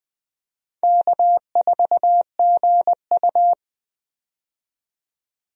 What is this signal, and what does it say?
Morse code, heard as a keyed tone, about 20 wpm, 700 Hz. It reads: K4GU
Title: USA Call Signs - 4 Characters - 20wpm